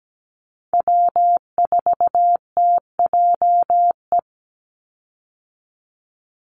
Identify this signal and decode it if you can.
Morse code, heard as a keyed tone, about 17 wpm, 700 Hz. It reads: W4TJE
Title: Call Signs - 17wpm